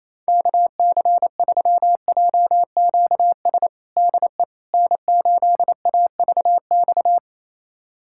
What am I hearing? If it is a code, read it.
Morse code, heard as a keyed tone, about 27 wpm, 700 Hz. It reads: KC3JQS DE N8AVX
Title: USA Callsign DE Callsign 28wpm